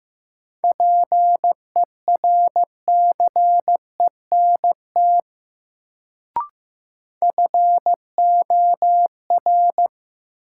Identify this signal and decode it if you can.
Morse code, heard as a keyed tone, about 15 wpm, 700 Hz. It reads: PERCENT FOR
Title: Sets of 2 Words - New Word Drill - 15wpm